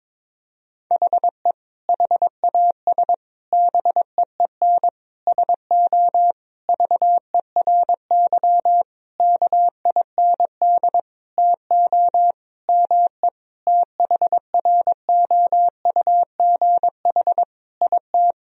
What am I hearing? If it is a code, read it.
Morse code, heard as a keyed tone, about 22 wpm, 700 Hz. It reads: HE HAS BEEN SO VERY KIND TO ME THROUGH IT
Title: Sentences from Top 1000 Words - Review Part 1 - 22wpm